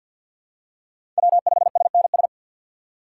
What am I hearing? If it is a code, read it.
Morse code, heard as a keyed tone, about 50 wpm, 700 Hz. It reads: W5SNS
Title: USA Call Signs - 5 Characters - 2x Character Spacing - 50wpm